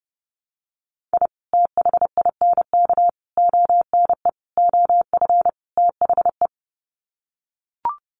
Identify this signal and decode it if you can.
Morse code, heard as a keyed tone, about 29 wpm, 700 Hz. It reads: I THINK ONE OF THE
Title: Sets of 5 Words 30wpm